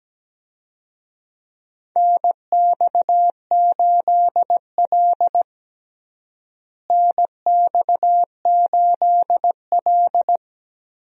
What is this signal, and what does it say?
Morse code, heard as a keyed tone, about 17 wpm, 700 Hz. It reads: NX8L NX8L
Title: USA Call Signs - 4 Characters - Repeated 2x - 17wpm